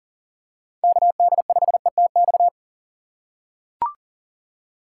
Morse code, heard as a keyed tone, about 40 wpm, 690 Hz. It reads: KD5ETX
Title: USA Call Signs - 6x Character Spacing - 40wpm